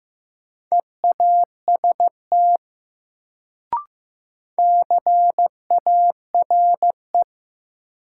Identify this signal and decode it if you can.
Morse code, heard as a keyed tone, about 15 wpm, 700 Hz. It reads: EAST CARE
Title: Sets of 3 Words - New Word Drill - 15wpm